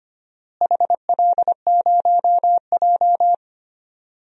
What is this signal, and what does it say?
Morse code, heard as a keyed tone, about 25 wpm, 700 Hz. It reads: HL0J
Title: DXCC-100 Call Signs - 4 Characters - Repeated 3x - 25wpm